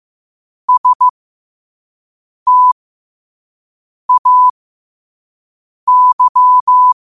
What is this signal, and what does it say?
Morse code, heard as a keyed tone, about 15 wpm, spaced as for 5 wpm, 1000 Hz. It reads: STAY
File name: qotdfiles / 5wpm